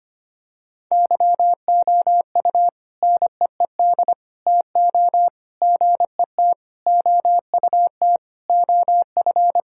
Morse code, heard as a keyed tone, about 25 wpm, 700 Hz. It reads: YOU NEED TO GET OUT OF
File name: practice / 25wpm / Sentences with 3-Word Phrases - 25wpm